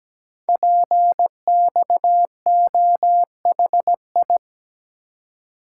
Morse code, heard as a keyed tone, about 17 wpm, 700 Hz. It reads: PXOHI
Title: Code Groups - 5 Letters - 17wpm